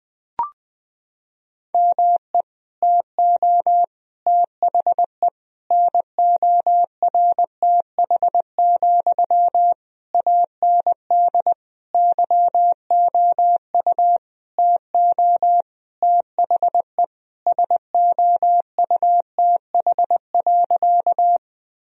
Morse code, heard as a keyed tone, about 20 wpm, 700 Hz. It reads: ME TO THE NORTH, AND YOU TO THE SOUTH.
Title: Sentences from Top 1000 Words - Review Part 2 - 20wpm